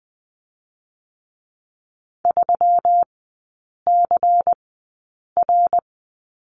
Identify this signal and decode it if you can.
Morse code, heard as a keyed tone, about 20 wpm, spaced as for 8 wpm, 700 Hz. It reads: 3CR
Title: 121 Lesson 20e - Number 3 - Three Characters 20wpm